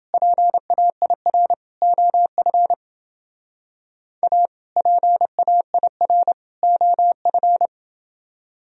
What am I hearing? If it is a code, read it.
Morse code, heard as a keyed tone, about 29 wpm, 700 Hz. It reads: PAIR OF A PAIR OF